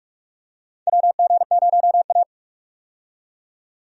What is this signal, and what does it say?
Morse code, heard as a keyed tone, about 46 wpm, 700 Hz. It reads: WG0A
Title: USA Call Signs - 4 Characters - 45wpm